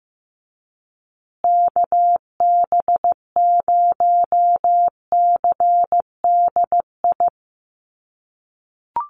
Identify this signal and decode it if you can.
Morse code, heard as a keyed tone, about 15 wpm, 700 Hz. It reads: KB0CDI
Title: USA Call Signs Repeated 2x 15wpm